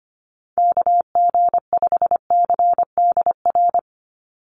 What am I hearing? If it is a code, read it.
Morse code, heard as a keyed tone, about 25 wpm, 700 Hz. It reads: KG5CDR
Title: USA Call Signs 25wpm